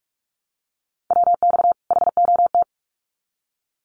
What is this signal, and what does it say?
Morse code, heard as a keyed tone, about 44 wpm, 700 Hz. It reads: WX HOT
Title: Common QSO Phrases 45wpm